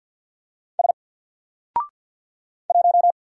Morse code, heard as a keyed tone, about 48 wpm, 700 Hz. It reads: S 1
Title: Single Character (Letter:Number:Punctuation:Prosign) Alt 50wpm